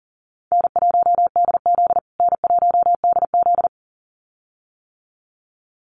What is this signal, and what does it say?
Morse code, heard as a keyed tone, about 40 wpm, 700 Hz. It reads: N1DZ N1DZ
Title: USA Call Signs Repeated 2x 40wpm